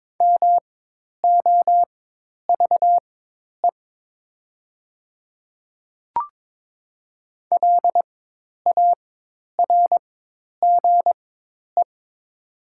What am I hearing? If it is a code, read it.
Morse code, heard as a keyed tone, about 22 wpm, spaced as for 10 wpm, 700 Hz. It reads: MOVE LARGE
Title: 4x Character Spacing - Top 300 Words - 22wpm